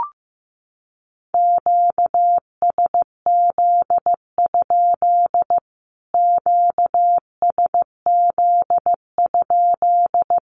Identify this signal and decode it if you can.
Morse code, heard as a keyed tone, about 14 wpm, 700 Hz. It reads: QSZ? QSZ?
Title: Q-Codes Repeated 3x 15wpm